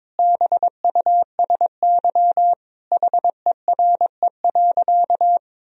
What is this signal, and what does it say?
Morse code, heard as a keyed tone, about 22 wpm, 700 Hz. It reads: BUSY HERE.
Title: Sentences from Top 1000 Words - Review Part 3 - 22wpm